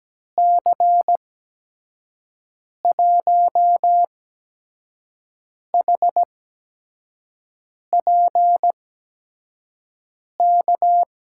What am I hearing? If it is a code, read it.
Morse code, heard as a keyed tone, about 17 wpm, spaced as for 5 wpm, 700 Hz. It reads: C1HPK
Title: USA Call Signs - 6 Characters - 8x Character Spacing - 17wpm